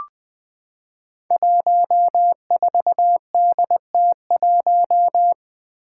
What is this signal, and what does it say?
Morse code, heard as a keyed tone, about 20 wpm, 700 Hz. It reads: 14DT1
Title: Five Challenging Letter-Number Combinations 20wpm